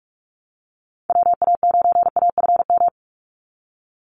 Morse code, heard as a keyed tone, about 46 wpm, 700 Hz. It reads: WA9AFM
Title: USA Call Signs 45wpm